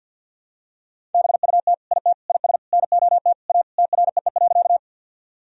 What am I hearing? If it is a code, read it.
Morse code, heard as a keyed tone, about 49 wpm, 700 Hz. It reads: BUT IT IS NOT A TREE.
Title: Sentences from Top 1000 Words - Review Part 1 - 50wpm